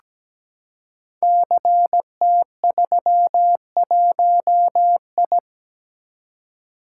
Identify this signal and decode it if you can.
Morse code, heard as a keyed tone, about 17 wpm, 690 Hz. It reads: CT31I
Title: Code Groups - 5 Letter:Numbers - 17wpm